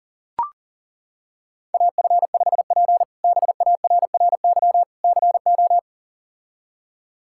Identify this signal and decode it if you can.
Morse code, heard as a keyed tone, about 40 wpm, 700 Hz. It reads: AF5P BARRY CO
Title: K1USN SST QSO Practice - 40wpm